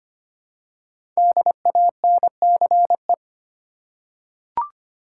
Morse code, heard as a keyed tone, about 25 wpm, 700 Hz. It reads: DANCE
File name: 5 Letter Words - First 3 Letters Given - 25wpm